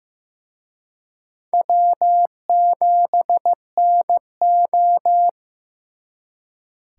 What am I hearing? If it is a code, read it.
Morse code, heard as a keyed tone, about 18 wpm, 700 Hz. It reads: W7NO
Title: Speed Racing - USA 1x2 Call Signs 15wpm